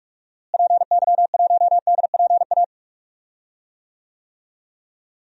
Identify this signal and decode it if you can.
Morse code, heard as a keyed tone, about 46 wpm, 700 Hz. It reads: PY1DPA
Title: DXCC-100 Call Signs - 6 Characters - Repeated 2x - 45wpm